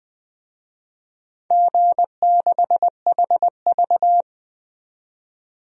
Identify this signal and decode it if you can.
Morse code, heard as a keyed tone, about 21 wpm, 700 Hz. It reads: G6HV
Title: DXCC-100 Call Signs - 4 Characters - 20wpm